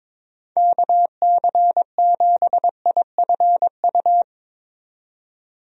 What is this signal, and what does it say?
Morse code, heard as a keyed tone, about 21 wpm, 700 Hz. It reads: KC7IFU
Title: USA Call Signs - 6 Characters - 22wpm